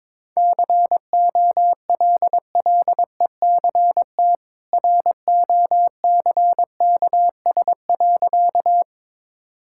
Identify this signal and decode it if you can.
Morse code, heard as a keyed tone, about 22 wpm, 690 Hz. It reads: COLLECT ROCKS.
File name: practice / 22wpm / Easy Sentences with Hints - 22wpm